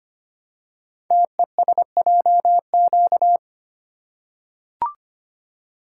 Morse code, heard as a keyed tone, about 25 wpm, 700 Hz. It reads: TESJQ
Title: Code Groups - 5 Letters - 25wpm